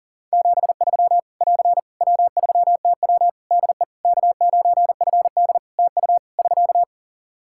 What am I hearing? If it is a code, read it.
Morse code, heard as a keyed tone, about 40 wpm, 700 Hz. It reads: 73 <AR> W3TW DE K9FD TU <SK>
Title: Practice QSO - Exchange 4 - Ending - 40wpm